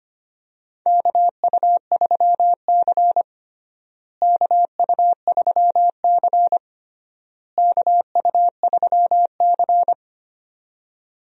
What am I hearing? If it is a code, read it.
Morse code, heard as a keyed tone, about 25 wpm, 700 Hz. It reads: KU3C KU3C KU3C
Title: USA Call Signs - 4 Characters - Repeated 3x - 25wpm